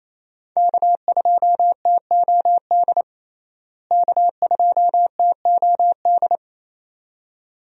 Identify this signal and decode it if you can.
Morse code, heard as a keyed tone, about 27 wpm, 700 Hz. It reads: K2TOD K2TOD
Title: USA Call Signs - 5 Characters - Repeated 2x - 28wpm